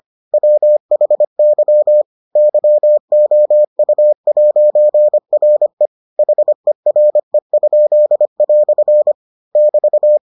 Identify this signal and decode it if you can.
Morse code, heard as a keyed tone, about 25 wpm, 600 Hz. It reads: WHY YOU'RE HERE?" <BT>